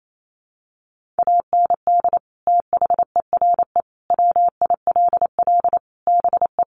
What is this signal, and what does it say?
Morse code, heard as a keyed tone, about 28 wpm, 700 Hz. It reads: AND THERE WILL BE
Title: Sentences from Top 1500 Words 28wpm